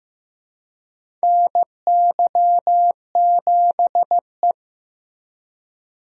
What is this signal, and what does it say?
Morse code, heard as a keyed tone, about 15 wpm, 700 Hz. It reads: NY7E
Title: USA Call Signs - 4 Characters - Repeated 2x - 15wpm